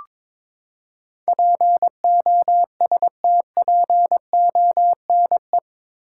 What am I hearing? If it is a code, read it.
Morse code, heard as a keyed tone, about 22 wpm, 700 Hz. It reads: POSTPONE
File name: practice / 22wpm / Eight Letter Words - 22wpm